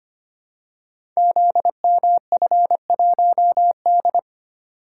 Morse code, heard as a keyed tone, about 25 wpm, 700 Hz. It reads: ZMF1D
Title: Code Groups - 5 Letter:Numbers - 25wpm